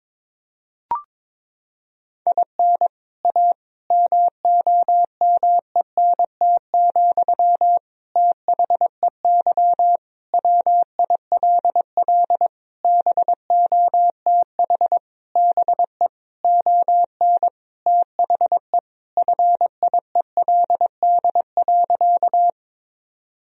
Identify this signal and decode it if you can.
Morse code, heard as a keyed tone, about 22 wpm, 700 Hz. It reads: IN A MOMENT, THEY WILL BOTH BE ON THE FIELD.
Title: Sentences from Top 1000 Words - Review Part 2 - 22wpm